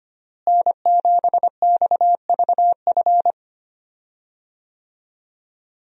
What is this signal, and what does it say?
Morse code, heard as a keyed tone, about 25 wpm, 700 Hz. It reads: N7XVF
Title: USA Call Signs - 5 Characters - Repeated 2x - 25wpm